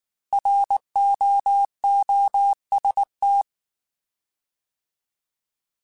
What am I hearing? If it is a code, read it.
Morse code, heard as a keyed tone, about 19 wpm, 800 Hz. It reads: ROOST
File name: Alphabetical / Alphabetical mp3s / Rs @18